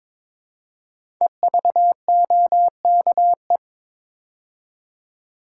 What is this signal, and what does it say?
Morse code, heard as a keyed tone, about 22 wpm, 700 Hz. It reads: EVOKE
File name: 5 Letter Words - First 3 Letters Given - 22wpm